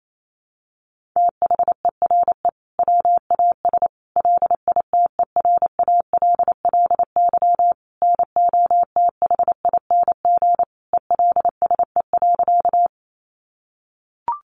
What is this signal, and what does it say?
Morse code, heard as a keyed tone, about 28 wpm, 700 Hz. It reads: THERE WAS LITERALLY NOTHING ELSE.
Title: Sentences from Top 1400 Words 28wpm